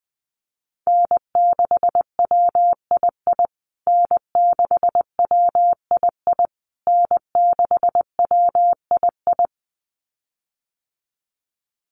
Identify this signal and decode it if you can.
Morse code, heard as a keyed tone, about 20 wpm, 700 Hz. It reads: N6WII N6WII N6WII
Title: USA Call Signs Repeated 3x 20wpm